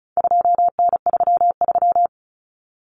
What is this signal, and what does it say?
Morse code, heard as a keyed tone, about 34 wpm, 700 Hz. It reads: 2N33
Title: Common Cut Numbers - 4-Digit 35wpm